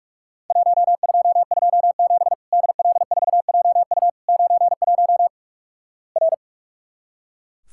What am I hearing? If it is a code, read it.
Morse code, heard as a keyed tone, about 44 wpm, 700 Hz. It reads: 1227 DL4JU 91 R
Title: Worked All Europe DX Contest - QTC Exchange - 45wpm